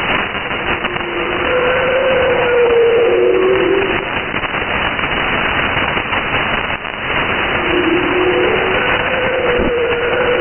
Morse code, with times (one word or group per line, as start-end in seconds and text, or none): none